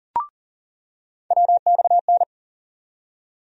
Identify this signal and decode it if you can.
Morse code, heard as a keyed tone, about 40 wpm, 700 Hz. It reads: WXN
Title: USA Call Sign Postfixes - 3 Letters 40wpm